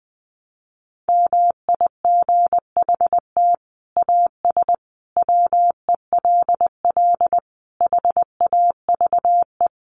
Morse code, heard as a keyed tone, about 20 wpm, 700 Hz. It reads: MIGHT AS WELL HAVE
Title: Sets of 5 Words Encore2 20wpm